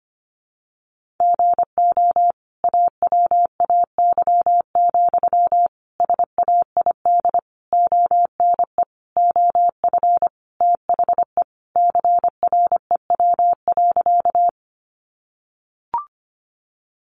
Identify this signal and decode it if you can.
Morse code, heard as a keyed tone, about 25 wpm, 700 Hz. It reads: GO AWAY, SAID ONE OF THE CREW.
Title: Sentences from Top 1500 Words 25wpm